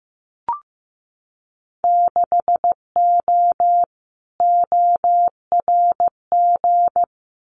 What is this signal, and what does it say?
Morse code, heard as a keyed tone, about 15 wpm, 700 Hz. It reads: 6O ORG
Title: Winter Field Day Exchanges Encore 15wpm